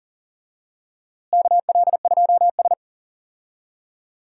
Morse code, heard as a keyed tone, about 40 wpm, 700 Hz. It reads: KL2S
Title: USA Call Signs - 4 Characters - Repeated 2x - 40wpm